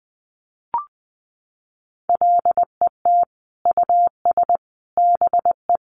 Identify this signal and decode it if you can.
Morse code, heard as a keyed tone, about 20 wpm, 700 Hz. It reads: LET US BE
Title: Sentences from Top 1400 Words 20wpm